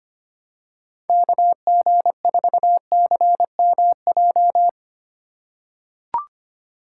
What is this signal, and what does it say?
Morse code, heard as a keyed tone, about 25 wpm, 700 Hz. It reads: KG4CMJ
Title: USA Call Signs - 6 Characters - 2x Character Spacing - 25wpm